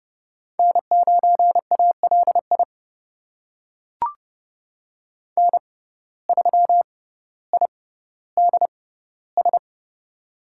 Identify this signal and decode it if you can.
Morse code, heard as a keyed tone, about 30 wpm, spaced as for 15 wpm, 700 Hz. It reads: N9ALI N3IDS
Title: USA Call Signs - 5 Characters - 6x Character Spacing - 30wpm